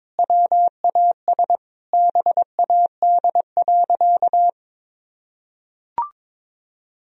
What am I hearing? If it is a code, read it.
Morse code, heard as a keyed tone, about 22 wpm, 700 Hz. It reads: WAS BAD.
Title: Sentences from Top 1000 Words - Review Part 1 - 22wpm